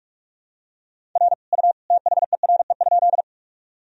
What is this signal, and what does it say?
Morse code, heard as a keyed tone, about 45 wpm, 700 Hz. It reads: R U THERE?
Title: Sentences from Top 100 Words and Abbreviations 45wpm